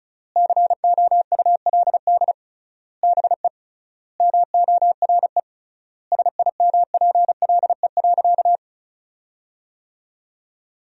Morse code, heard as a keyed tone, about 34 wpm, 700 Hz. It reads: COULD BE MORE SIMPLE.
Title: 3x Word-Spacing - Sentences from Top 500 Words - 35wpm